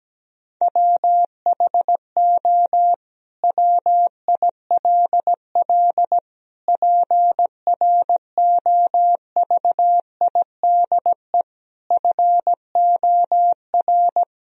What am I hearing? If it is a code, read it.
Morse code, heard as a keyed tone, about 17 wpm, 700 Hz. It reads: WHO WILL PROVIDE FOR
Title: Sentences from Top 1000 Words - Review Part 3 - 17wpm